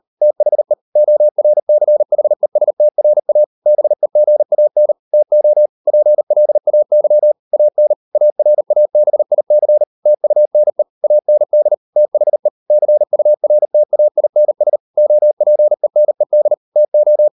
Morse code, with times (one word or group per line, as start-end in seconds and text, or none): 0.2-0.7 THE
1.0-3.4 ORCHESTRA
3.7-4.9 BEGAN
5.1-5.7 TO
5.9-7.3 PLAY
7.5-7.9 AN
8.1-9.8 ARABIC
10.1-10.8 TUNE
11.0-11.7 AND
12.0-12.5 THE
12.7-14.8 CURTAINS
15.0-16.5 OPENED
16.8-17.3 TO